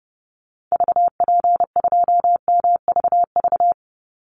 Morse code, heard as a keyed tone, about 30 wpm, 700 Hz. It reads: VP2MVV
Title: Call Signs Repeated 3x - 30wpm